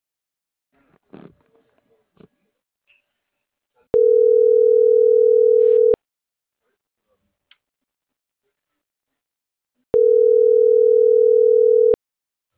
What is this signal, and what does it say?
Ringback tone (North America), 440 + 480 Hz.